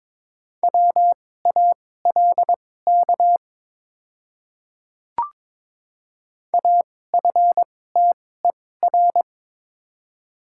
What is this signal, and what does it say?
Morse code, heard as a keyed tone, about 21 wpm, spaced as for 16 wpm, 700 Hz. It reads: WALK AFTER
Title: 2x Character Spacing - Top 500 Words - 22wpm